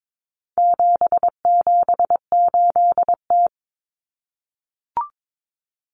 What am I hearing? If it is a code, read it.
Morse code, heard as a keyed tone, about 22 wpm, 700 Hz. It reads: 778T
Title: Common Cut Numbers - 4-Digit 22wpm